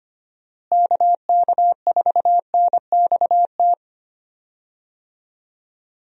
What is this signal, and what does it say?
Morse code, heard as a keyed tone, about 25 wpm, 700 Hz. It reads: KK4NXT
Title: USA Call Signs - 6 Characters - Repeated 2x - 25wpm